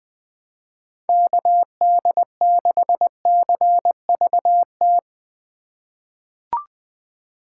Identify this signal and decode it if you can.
Morse code, heard as a keyed tone, about 20 wpm, 700 Hz. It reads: KD6CVT
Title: USA Call Signs - 6 Characters - 3x Character Spacing - 20wpm